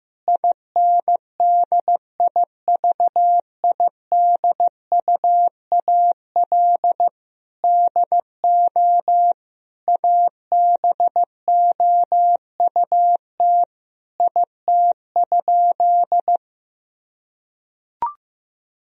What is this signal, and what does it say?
Morse code, heard as a keyed tone, about 16 wpm, 700 Hz. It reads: INDIVIDUAL DO ABOUT IT?
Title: Sentences from Top 1000 Words - Review Part 4 - 15wpm